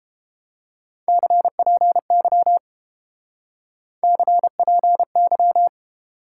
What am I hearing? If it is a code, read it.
Morse code, heard as a keyed tone, about 30 wpm, 700 Hz. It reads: CPY CPY